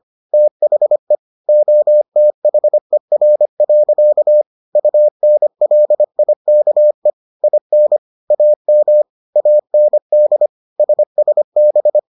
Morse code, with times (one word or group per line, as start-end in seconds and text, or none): 0.3-1.1 THE
1.5-4.4 OTHER.
4.8-7.1 UNLIKE
7.4-8.0 IN
8.3-9.0 AM
9.4-10.5 AND
10.8-12.0 SSB